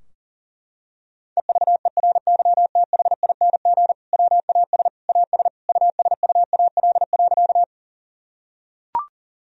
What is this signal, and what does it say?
Morse code, heard as a keyed tone, about 40 wpm, 700 Hz. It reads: EVERYTHING WAS AS USUAL.
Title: Sentences from Top 1000 Words - Review Part 2 - 40wpm